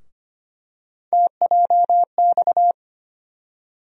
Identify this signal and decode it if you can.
Morse code, heard as a keyed tone, about 25 wpm, 700 Hz. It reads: TJX
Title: USA Call Sign Postfixes - 3 Letters 25wpm